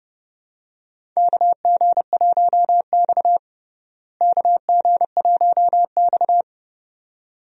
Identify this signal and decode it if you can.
Morse code, heard as a keyed tone, about 30 wpm, 700 Hz. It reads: KG1X KG1X
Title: USA Call Signs - 4 Characters - Repeated 2x - 30wpm